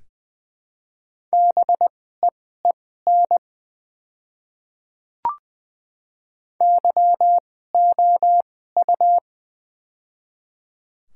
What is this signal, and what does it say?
Morse code, heard as a keyed tone, about 20 wpm, spaced as for 14 wpm, 700 Hz. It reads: BEEN YOU
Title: 2x Character Spacing - Top 200 Words - 20wpm